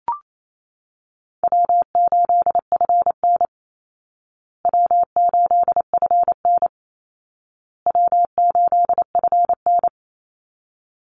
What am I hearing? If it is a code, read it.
Morse code, heard as a keyed tone, about 28 wpm, 700 Hz. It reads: W8FN W8FN W8FN
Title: Easy Call Signs Repeated 3x - 28wpm